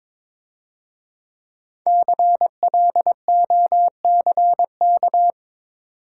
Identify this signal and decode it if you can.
Morse code, heard as a keyed tone, about 25 wpm, 690 Hz. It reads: CLOCK
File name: Easy Words - Speed-Racing with Hints - 22wpm